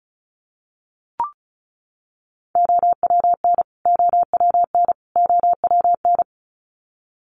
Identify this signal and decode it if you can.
Morse code, heard as a keyed tone, about 34 wpm, 700 Hz. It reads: OWN OWN OWN
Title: Top 500 Words Repeated 35wpm